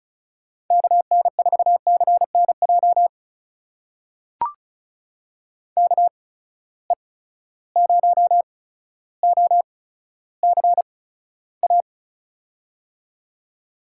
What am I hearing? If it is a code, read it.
Morse code, heard as a keyed tone, about 36 wpm, spaced as for 15 wpm, 700 Hz. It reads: KN4CNJ KE0OCA
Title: USA Call Signs - 6 Characters - 8x Character Spacing - 35wpm